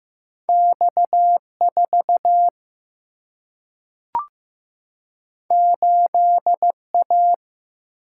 Morse code, heard as a keyed tone, about 15 wpm, 700 Hz. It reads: X 4 8 A
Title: Single-Letter Single-Number 15wpm